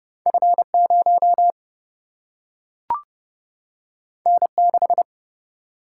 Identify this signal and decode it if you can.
Morse code, heard as a keyed tone, about 30 wpm, 700 Hz. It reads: F0 N6
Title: Single-Letter Single-Number 30wpm